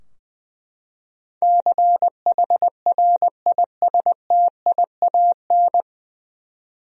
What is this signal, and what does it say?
Morse code, heard as a keyed tone, about 20 wpm, 700 Hz. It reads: CHRISTIAN
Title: USA Common Names - 20wpm